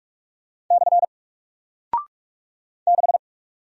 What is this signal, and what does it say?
Morse code, heard as a keyed tone, about 44 wpm, 700 Hz. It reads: / 6
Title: Single Character (Letter:Number:Punctuation:Prosign) Alt 45wpm